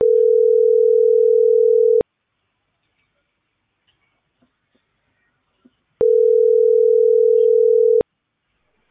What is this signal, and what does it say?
Ringback tone (North America), 440 + 480 Hz.